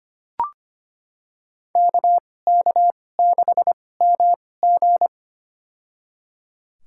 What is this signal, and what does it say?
Morse code, heard as a keyed tone, about 25 wpm, spaced as for 18 wpm, 700 Hz. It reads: KK6MG
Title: USA Call Signs - 5 Characters - 2x Character Spacing - 25wpm